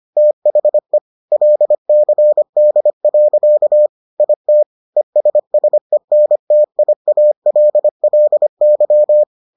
Morse code, heard as a keyed tone, about 25 wpm, 600 Hz. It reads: THE LCD. IT ESSENTIALLY